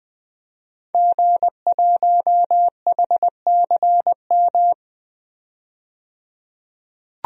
Morse code, heard as a keyed tone, about 20 wpm, 700 Hz. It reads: G1HCM
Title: DXCC-100 Call Signs - 5 Characters - 20wpm